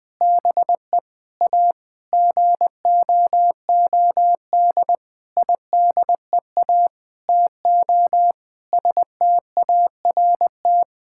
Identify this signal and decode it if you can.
Morse code, heard as a keyed tone, about 20 wpm, 700 Hz. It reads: BE A GOOD IDEA TO START
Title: Sentences with 8-Word Phrases - 20wpm